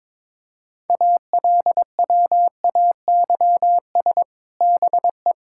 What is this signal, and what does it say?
Morse code, heard as a keyed tone, about 22 wpm, 700 Hz. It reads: ALWAYS BE
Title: Sentences from Top 1600 Words 22wpm